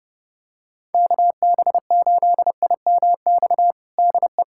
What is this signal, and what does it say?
Morse code, heard as a keyed tone, about 32 wpm, 700 Hz. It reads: KB8IMX DE
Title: USA Callsign DE Callsign 30wpm